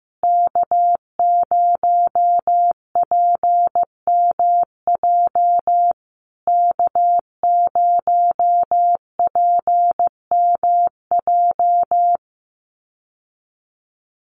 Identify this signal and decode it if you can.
Morse code, heard as a keyed tone, about 15 wpm, 700 Hz. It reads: K0PMJ K0PMJ
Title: USA Call Signs Repeated 3x 15wpm